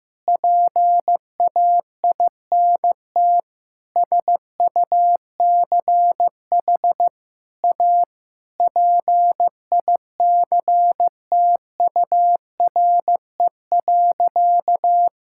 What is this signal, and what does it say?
Morse code, heard as a keyed tone, about 15 wpm, 700 Hz. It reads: PAINT SUCH A PICTURE.
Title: Sentences from Top 1000 Words - Review Part 6 - 15wpm